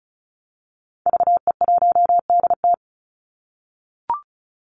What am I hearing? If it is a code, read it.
Morse code, heard as a keyed tone, about 34 wpm, 700 Hz. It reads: VE1DT
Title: Call Signs - 35wpm